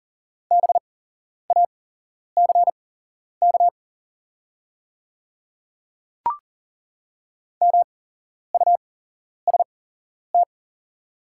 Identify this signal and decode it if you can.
Morse code, heard as a keyed tone, about 40 wpm, spaced as for 11 wpm, 700 Hz. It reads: BACK MUST
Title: 8x Character Spacing - Top 200 Words - 40wpm